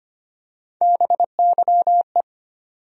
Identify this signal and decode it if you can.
Morse code, heard as a keyed tone, about 25 wpm, 700 Hz. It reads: BYE
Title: USA Call Sign Postfixes - 3 Letters 25wpm